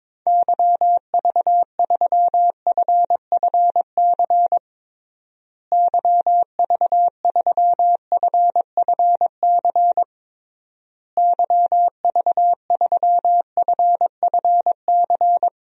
Morse code, heard as a keyed tone, about 22 wpm, 700 Hz. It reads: YV3FFC YV3FFC YV3FFC
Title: DXCC-100 Call Signs - 6 Characters - Repeated 3x - 22wpm